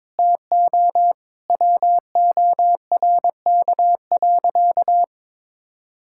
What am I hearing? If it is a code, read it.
Morse code, heard as a keyed tone, about 22 wpm, 700 Hz. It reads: TO WORK.
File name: Sentences with 9-Word Phrases - 22wpm